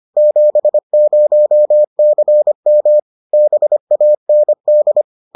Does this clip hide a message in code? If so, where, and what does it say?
Yes: Morse code from 0.2 to 5.0 s, heard as a keyed tone, about 25 wpm, 600 Hz. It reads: 70CM BAND